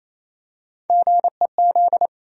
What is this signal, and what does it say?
Morse code, heard as a keyed tone, about 28 wpm, 700 Hz. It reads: GEZ
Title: USA Call Sign Postfixes - 3 Letters 28wpm